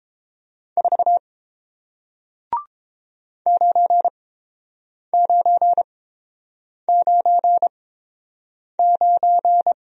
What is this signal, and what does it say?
Morse code, heard as a keyed tone, about 30 wpm, spaced as for 7 wpm, 700 Hz. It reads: 4 9999